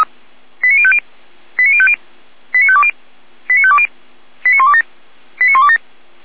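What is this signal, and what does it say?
Five-tone selective call (ZVEI), heard as quick runs of five tones, tones of about 70 ms. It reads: 89050 89050 89520 89520 89127 89127
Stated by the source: ZVEI five-tone selective call received on SDR.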